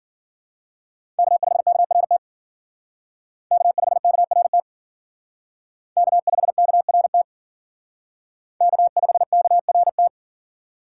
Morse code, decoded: K5KRT K5KRT K5KRT K5KRT

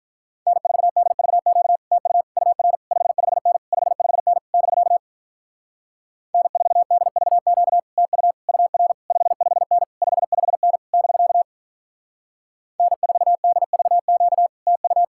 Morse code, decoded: N4DVQ TU UR 55N 55N <BK> N4DVQ TU UR 55N 55N <BK> N4DVQ TU

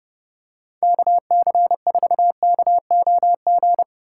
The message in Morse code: KC4KOG

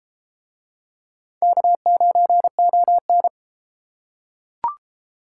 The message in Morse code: K9ON

